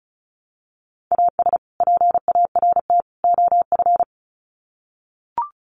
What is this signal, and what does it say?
Morse code, heard as a keyed tone, about 34 wpm, 700 Hz. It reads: AS PART OF
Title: Sets of 3 Words 35wpm